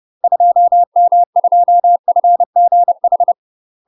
Morse code, heard as a keyed tone, about 30 wpm, 700 Hz. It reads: 2M2FGH
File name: UK MASTER QSOs TRAINING / 30WPM / UK MASTER QSOs 0031